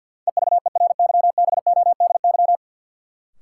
Morse code, heard as a keyed tone, about 51 wpm, 700 Hz. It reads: EVERYBODY
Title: Sentences from Top 1400 Words - New Word Drill - 50wpm